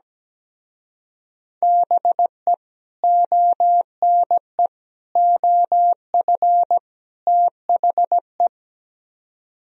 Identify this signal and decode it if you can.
Morse code, heard as a keyed tone, about 18 wpm, 700 Hz. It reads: BE ONE OF THE
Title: Speed-Racing - Sets of 4 Words 17wpm